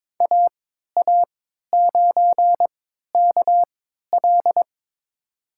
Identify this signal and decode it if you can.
Morse code, heard as a keyed tone, about 22 wpm, spaced as for 13 wpm, 700 Hz. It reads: AA9KL
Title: USA Call Signs - 5 Characters - 3x Character Spacing - 22wpm